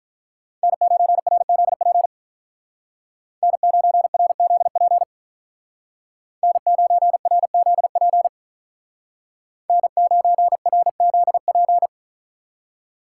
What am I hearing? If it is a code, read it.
Morse code, heard as a keyed tone, about 44 wpm, 700 Hz. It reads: N9RZP N9RZP N9RZP N9RZP
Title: Speed Racing - USA 1x3 Call Signs 35wpm